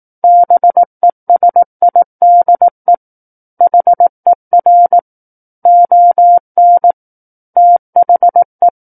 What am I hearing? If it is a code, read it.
Morse code, heard as a keyed tone, about 18 wpm, 700 Hz. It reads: BESIDE HER ON THE
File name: 18 WPM